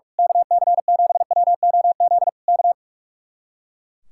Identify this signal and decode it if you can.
Morse code, heard as a keyed tone, about 43 wpm, 700 Hz. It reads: KC7WOZ K
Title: Speed-Racing Common QSO Phrases Encore - 45wpm